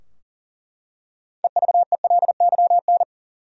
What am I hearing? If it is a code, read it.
Morse code, heard as a keyed tone, about 40 wpm, 700 Hz. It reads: EVELYN
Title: USA Common Names - 40wpm